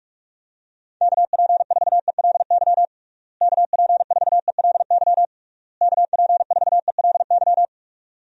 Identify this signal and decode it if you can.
Morse code, heard as a keyed tone, about 46 wpm, 700 Hz. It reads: KP4ELY KP4ELY KP4ELY
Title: USA Call Signs - 6 Characters - Repeated 3x - 45wpm